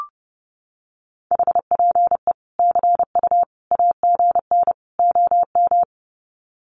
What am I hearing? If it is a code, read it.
Morse code, heard as a keyed tone, about 30 wpm, 700 Hz. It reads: HPE CU AGN OM
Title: Common QSO Phrases 30wpm